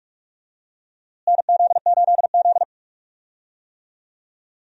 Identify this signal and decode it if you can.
Morse code, heard as a keyed tone, about 46 wpm, 700 Hz. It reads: NZ8Z
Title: USA Call Signs - 4 Characters - Repeated 2x - 45wpm